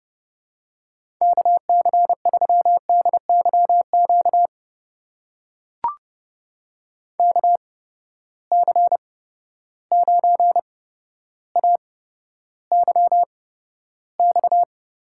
Morse code, heard as a keyed tone, about 30 wpm, spaced as for 13 wpm, 700 Hz. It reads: KC3DYQ KC9AYX
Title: USA Call Signs - 6 Characters - 8x Character Spacing - 30wpm